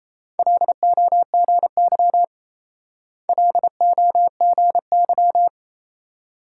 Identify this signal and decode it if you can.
Morse code, heard as a keyed tone, about 31 wpm, 700 Hz. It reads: LOGY LOGY
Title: Speed-Racing Multi-letter Phonemes - 28wpm